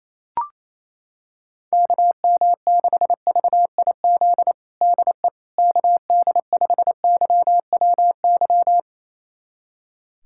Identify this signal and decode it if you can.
Morse code, heard as a keyed tone, about 28 wpm, 700 Hz. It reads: KM6VIZ DE KD5YWY
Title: USA Callsign DE Callsign - 6 Characters - 28wpm